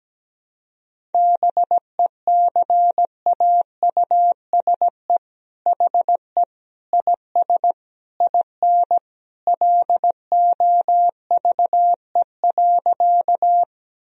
Morse code, heard as a keyed tone, about 17 wpm, 700 Hz. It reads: BECAUSE HE IS IN LOVE.
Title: Sentences from Top 1000 Words - Review Part 3 - 17wpm